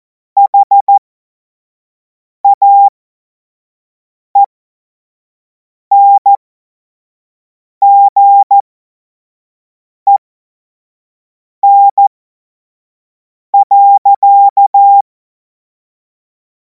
Morse code, heard as a keyed tone, about 14 wpm, spaced as for 5 wpm, 800 Hz. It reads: HAENGEN.
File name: Morsekurs / Witze / Witz 03. Februar 2026 25 F70